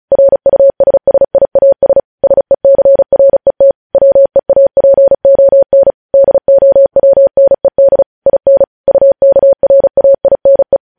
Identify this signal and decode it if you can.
Morse code, heard as a keyed tone, about 36 wpm, 550 Hz. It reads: RUSSIAS SECRET WEAPON DOWNED IN UKRAINE